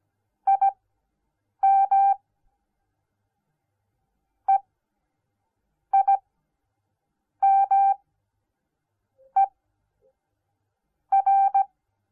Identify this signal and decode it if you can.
Morse code, heard as a keyed tone, about 17 wpm, spaced as for 6 wpm, 780 Hz. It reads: IM EIMER